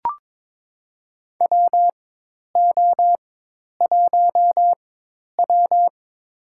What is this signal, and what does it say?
Morse code, heard as a keyed tone, about 21 wpm, spaced as for 10 wpm, 700 Hz. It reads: WO1W
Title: USA Call Signs - 4 Characters - 4x Character Spacing - 22wpm